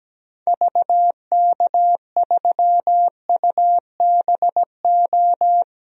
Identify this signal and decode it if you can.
Morse code, heard as a keyed tone, about 17 wpm, 700 Hz. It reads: VK3UBO
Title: DXCC-100 Call Signs - 6 Characters - Repeated 2x - 17wpm